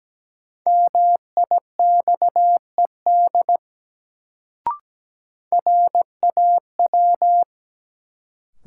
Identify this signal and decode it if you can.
Morse code, heard as a keyed tone, about 17 wpm, 700 Hz. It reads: MIXED RAW
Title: Sentences from Top 1500 Words - New Word Drill Alt - 17wpm